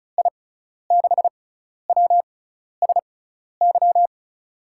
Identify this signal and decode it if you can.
Morse code, heard as a keyed tone, about 36 wpm, spaced as for 12 wpm, 700 Hz. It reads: I6WSY
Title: USA Call Signs - 6 Characters - 6x Character Spacing - 35wpm